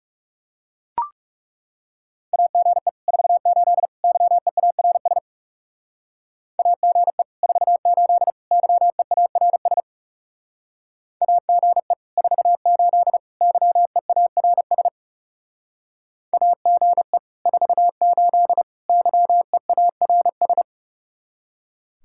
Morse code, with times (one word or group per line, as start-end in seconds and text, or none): 2.3-2.9 AGE
3.1-3.9 48
4.0-5.2 YEARS
6.6-7.2 AGE
7.4-8.3 48
8.5-9.8 YEARS
11.2-11.9 AGE
12.2-13.2 48
13.4-14.9 YEARS
16.3-17.2 AGE
17.5-18.6 48
18.9-20.6 YEARS